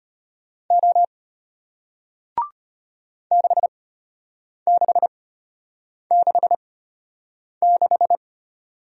Morse code, heard as a keyed tone, about 33 wpm, spaced as for 8 wpm, 700 Hz. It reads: O 6666